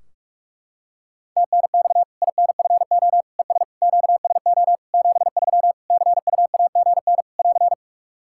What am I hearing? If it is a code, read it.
Morse code, heard as a keyed tone, about 44 wpm, 700 Hz. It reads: TNX INFO ES QSO 73 CUAGN <AR>
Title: Practice QSO - Exchange 4 - Ending - 45wpm